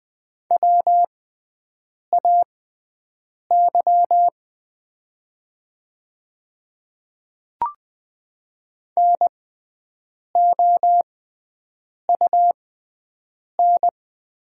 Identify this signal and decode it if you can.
Morse code, heard as a keyed tone, about 20 wpm, spaced as for 7 wpm, 700 Hz. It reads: WAY NOUN
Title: 6x Character Spacing - Top 500 Words - 20wpm